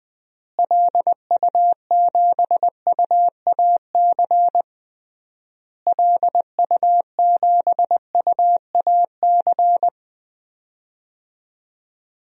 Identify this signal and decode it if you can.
Morse code, heard as a keyed tone, about 21 wpm, 700 Hz. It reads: LU7UAC LU7UAC
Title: DXCC-100 Call Signs - 6 Characters - Repeated 3x - 20wpm